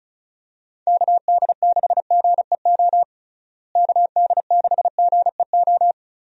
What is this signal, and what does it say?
Morse code, heard as a keyed tone, about 36 wpm, 700 Hz. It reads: KD6GEO KD6GEO
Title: USA Call Signs - 6 Characters - Repeated 2x - 35wpm